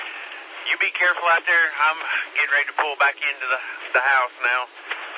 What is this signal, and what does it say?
Emergency Alert System (SAME) tones, heard as harsh buzzing bursts, with speech. It is heard from 0.6 to 4.2 s.